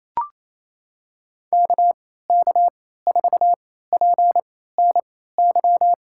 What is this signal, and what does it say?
Morse code, heard as a keyed tone, about 28 wpm, spaced as for 16 wpm, 700 Hz. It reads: KK4PNY
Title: USA Call Signs - 6 Characters - 3x Character Spacing - 28wpm